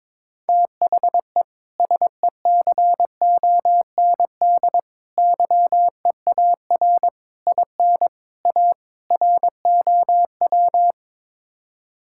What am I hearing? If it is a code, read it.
Morse code, heard as a keyed tone, about 22 wpm, 700 Hz. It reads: THE SECOND YEAR IN A ROW
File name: practice / 22wpm / Sets of 6 Words - 22wpm